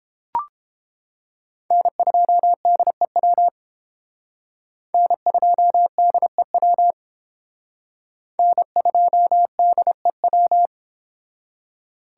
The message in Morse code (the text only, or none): N2DEW N2DEW N2DEW